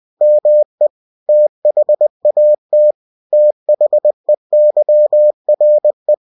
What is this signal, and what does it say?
Morse code, heard as a keyed tone, about 20 wpm, 600 Hz. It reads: ME THAT THEYRE
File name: cw-books / winnie-the-pooh / winnie-20wpm-page-0007